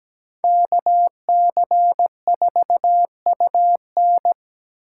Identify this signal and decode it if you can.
Morse code, heard as a keyed tone, about 20 wpm, 700 Hz. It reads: KC4UN
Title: Speed Racing - USA 2x2 Call Signs 17wpm